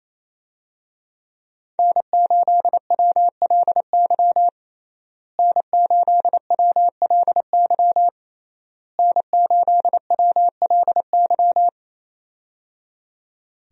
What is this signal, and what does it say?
Morse code, heard as a keyed tone, about 28 wpm, 700 Hz. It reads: N8WLY N8WLY N8WLY
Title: USA Call Signs - 5 Characters - Repeated 3x - 28wpm